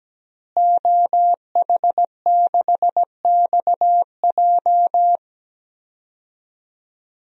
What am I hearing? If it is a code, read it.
Morse code, heard as a keyed tone, about 17 wpm, 700 Hz. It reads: OH6XJ
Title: DXCC-100 Call Signs - 5 Characters - Repeated 2x - 17wpm